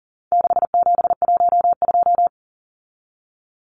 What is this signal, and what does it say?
Morse code, heard as a keyed tone, about 40 wpm, 700 Hz. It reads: 6712
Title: Numbers - 4-Digit 40wpm